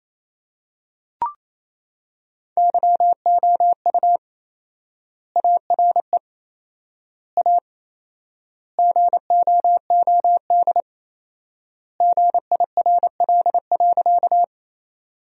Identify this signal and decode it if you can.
Morse code, heard as a keyed tone, about 28 wpm, 700 Hz. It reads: YOU ARE A GOOD GIRL.
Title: 4x Word-Spacing - Sentences from Top 400 Words - 28wpm